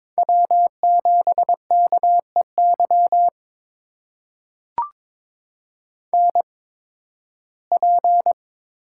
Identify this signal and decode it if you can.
Morse code, heard as a keyed tone, about 22 wpm, spaced as for 14 wpm, 700 Hz. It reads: W7KEY NP
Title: USA Call Signs - 5 Characters - 8x Character Spacing - 22wpm